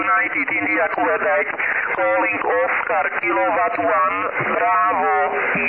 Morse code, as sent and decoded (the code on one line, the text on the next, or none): none